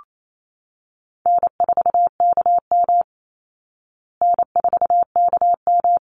Morse code, heard as a keyed tone, about 28 wpm, 700 Hz. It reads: N4KM N4KM
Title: Call Signs Repeated 2x - 28wpm